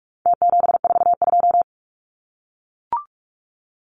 Morse code, heard as a keyed tone, about 44 wpm, 700 Hz. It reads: T742
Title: Numbers - Contest Style - 4-Digit 45wpm